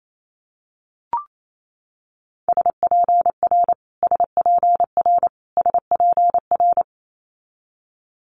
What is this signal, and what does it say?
Morse code, heard as a keyed tone, about 28 wpm, 700 Hz. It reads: SPR SPR SPR
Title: Multi-letter Phonemes Repeated 3x 28wpm